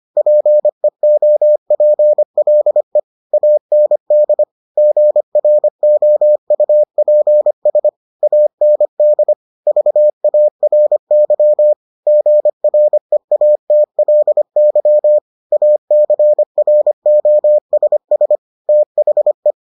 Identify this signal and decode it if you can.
Morse code, heard as a keyed tone, about 25 wpm, 600 Hz. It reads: PEOPLE AND GROUPS AND VARY GREATLY ACROSS THE